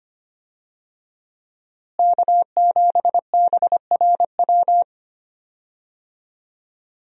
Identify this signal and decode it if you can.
Morse code, heard as a keyed tone, about 25 wpm, 700 Hz. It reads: K7BRW
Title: USA Call Signs - 5 Characters - Repeated 3x - 25wpm